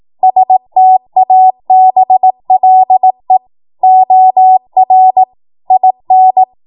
Morse code, heard as a keyed tone, about 18 wpm, 750 Hz. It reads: STABLE OR IN
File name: Morse / 18WPM